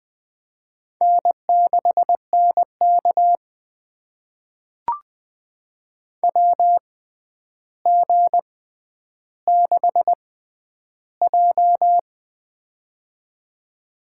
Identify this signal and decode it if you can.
Morse code, heard as a keyed tone, about 20 wpm, spaced as for 10 wpm, 700 Hz. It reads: N6NK WG6J
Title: USA Call Signs - 4 Characters - 6x Character Spacing - 20wpm